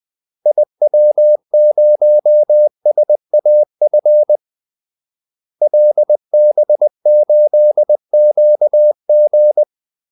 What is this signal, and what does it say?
Morse code, heard as a keyed tone, about 20 wpm, 600 Hz. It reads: IW0SAF LB8QG
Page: Kaldesignaler 20 wpm